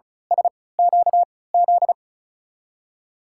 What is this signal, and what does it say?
Morse code, heard as a keyed tone, about 36 wpm, spaced as for 20 wpm, 700 Hz. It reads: SQZ
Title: USA Call Signs - 3x Character Spacing - 35wpm